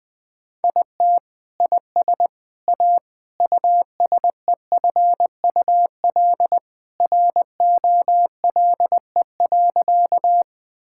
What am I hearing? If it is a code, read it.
Morse code, heard as a keyed tone, about 21 wpm, 700 Hz. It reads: IT IS A USEFUL ROLE.
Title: Sentences from Top 1000 Words - Review Part 6 - 20wpm